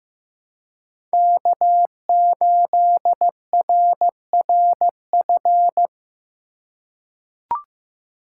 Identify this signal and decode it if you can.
Morse code, heard as a keyed tone, about 16 wpm, 700 Hz. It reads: K8RRF
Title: USA Call Signs - 5 Characters - 4x Character Spacing - 15wpm